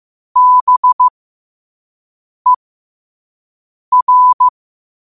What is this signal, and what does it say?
Morse code, heard as a keyed tone, about 15 wpm, spaced as for 5 wpm, 1000 Hz. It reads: BER